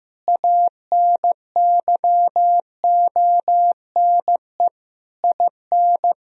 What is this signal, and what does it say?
Morse code, heard as a keyed tone, about 15 wpm, 700 Hz. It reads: ANYONE IN
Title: Sentences from Top 1000 Words - Review Part 4 - 15wpm